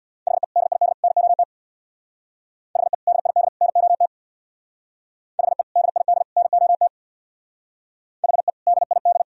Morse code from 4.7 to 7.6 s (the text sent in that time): HE DID NOT